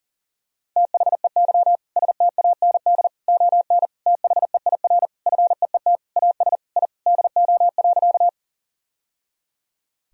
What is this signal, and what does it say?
Morse code, heard as a keyed tone, about 40 wpm, 700 Hz. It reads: THEY STAND ON THEIR FEET AS I DO.
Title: Sentences from Top 1000 Words - Review Part 1 - 40wpm